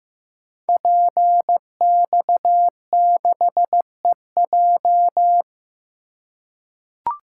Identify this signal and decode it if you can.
Morse code, heard as a keyed tone, about 15 wpm, 700 Hz. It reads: PX6EJ
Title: Code Groups - 5 Letter:Numbers - 15wpm